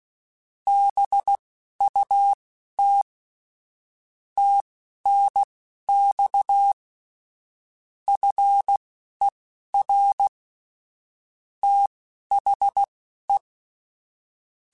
Morse code, heard as a keyed tone, about 16 wpm, spaced as for 11 wpm, 800 Hz. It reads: BUT TNX FER THE
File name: QSOs / QSOs-mp3s / kc7haa@10